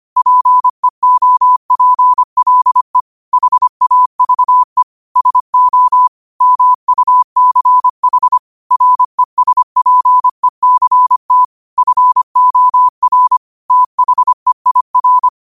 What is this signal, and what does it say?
Morse code, heard as a keyed tone, about 25 wpm, 1000 Hz. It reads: PEOPLE HAVE SO MUCH RESPECT FOR THEIR